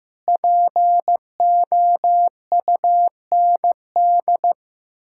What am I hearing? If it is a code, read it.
Morse code, heard as a keyed tone, about 15 wpm, 700 Hz. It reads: POUND
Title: Pangram Challenge v2 - 15wpm